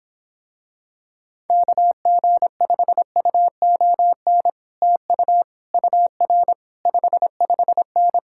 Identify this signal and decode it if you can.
Morse code, heard as a keyed tone, about 26 wpm, 700 Hz. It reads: KG5UON TU UR 55N